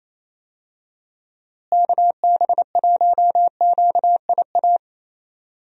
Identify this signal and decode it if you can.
Morse code, heard as a keyed tone, about 28 wpm, 700 Hz. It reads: KB1QIA
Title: USA Call Signs - 6 Characters - Repeated 3x - 28wpm